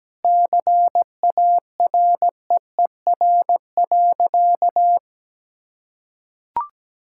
Morse code, heard as a keyed tone, about 17 wpm, 700 Hz. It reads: CAREER.
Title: Sentences from Top 1000 Words - Review Part 6 - 17wpm